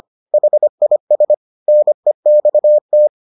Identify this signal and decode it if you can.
Morse code, heard as a keyed tone, about 25 wpm, 600 Hz. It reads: HIS NEXT